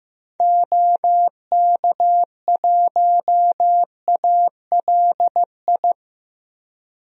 Morse code, heard as a keyed tone, about 15 wpm, 700 Hz. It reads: OK1ALI
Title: DXCC-100 Call Signs - 6 Characters - Repeated 3x - 15wpm